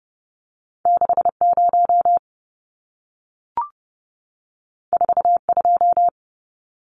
Morse code, heard as a keyed tone, about 30 wpm, 700 Hz. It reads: 60 42
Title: Numbers - 2-Digit 30wpm